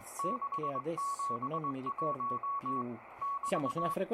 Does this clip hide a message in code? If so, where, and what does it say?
Yes: Morse code from 0.2 to 3.9 s, heard as a keyed tone, about 20 wpm, 1120 Hz, with speech and radio noise. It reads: 671 <BT>